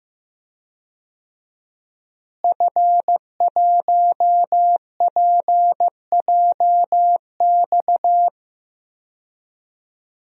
Morse code, heard as a keyed tone, about 15 wpm, 700 Hz. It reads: F1PJX
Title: DXCC-100 Call Signs - 5 Characters - Repeated 3x - 15wpm